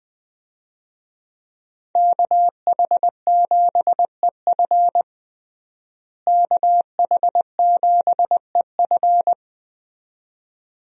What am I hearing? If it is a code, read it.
Morse code, heard as a keyed tone, about 20 wpm, 700 Hz. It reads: KH7EF KH7EF
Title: USA Call Signs - 5 Characters - Repeated 2x - 20wpm